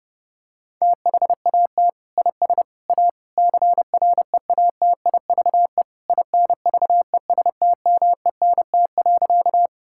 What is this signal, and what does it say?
Morse code, heard as a keyed tone, about 30 wpm, 700 Hz. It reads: THAT IS A CREATIVE INVESTMENT.
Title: Sentences from Top 1000 Words - Review Part 5 - 30wpm